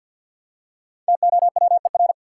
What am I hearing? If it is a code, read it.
Morse code, heard as a keyed tone, about 48 wpm, 700 Hz. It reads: TOWER
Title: Five Letter Words - 50wpm